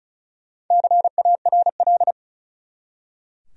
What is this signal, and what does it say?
Morse code, heard as a keyed tone, about 36 wpm, 700 Hz. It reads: CARL
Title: USA Common Names - 35wpm